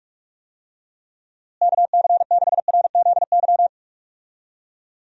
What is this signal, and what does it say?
Morse code, heard as a keyed tone, about 46 wpm, 700 Hz. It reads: KC6RZY
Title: USA Call Signs - 6 Characters - Repeated 3x - 45wpm